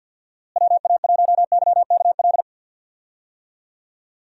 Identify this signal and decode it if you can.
Morse code, heard as a keyed tone, about 52 wpm, 700 Hz. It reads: WA1YKL
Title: USA Call Signs - 6 Characters - Repeated 2x - 50wpm